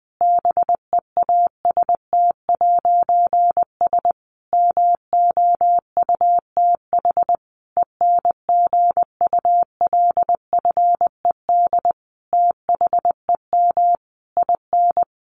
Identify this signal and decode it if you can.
Morse code, heard as a keyed tone, about 20 wpm, 700 Hz. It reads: BEAST'S MOUTH ENGULFED THEM IN
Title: A Sound Of Thunder 20wpm